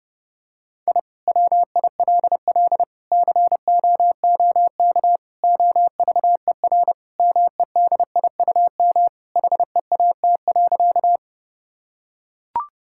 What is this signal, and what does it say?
Morse code, heard as a keyed tone, about 32 wpm, 700 Hz. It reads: I WILL COOK OVER MEDIUM HEAT.
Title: Sentences from Top 1000 Words - Review Part 6 - 30wpm